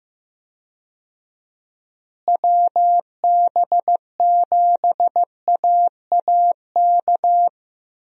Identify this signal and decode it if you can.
Morse code, heard as a keyed tone, about 15 wpm, 700 Hz. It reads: WB7AAK
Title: USA Call Signs - 6 Characters - Repeated 2x - 15wpm